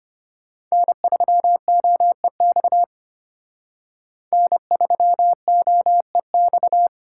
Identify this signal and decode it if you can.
Morse code, heard as a keyed tone, about 32 wpm, 700 Hz. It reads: N3OEX N3OEX